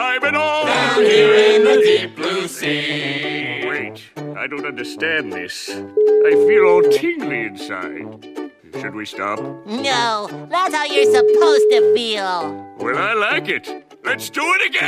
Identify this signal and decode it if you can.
Ringback tone (Europe, 425 Hz), with speech.